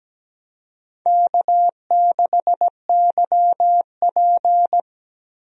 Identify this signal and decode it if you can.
Morse code, heard as a keyed tone, about 17 wpm, 700 Hz. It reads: K6YP
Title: USA Call Signs - 4 Characters - Repeated 2x - 17wpm